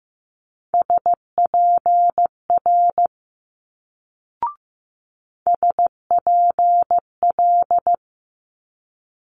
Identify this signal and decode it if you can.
Morse code, heard as a keyed tone, about 15 wpm, 700 Hz. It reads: SPR SPL
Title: Multi-letter Phonemes Quick Fire 15wpm